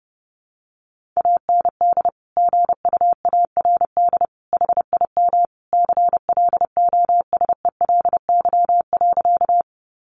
Morse code, heard as a keyed tone, about 30 wpm, 700 Hz. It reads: AND GUARD HIM CLOSELY.
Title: Sentences from Top 1400 Words 30wpm